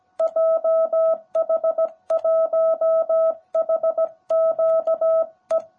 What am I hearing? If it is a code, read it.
Morse code, heard as a keyed tone, about 17 wpm, 660 Hz. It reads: JH1HQE
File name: jh8cbh / 2021cw / cw18b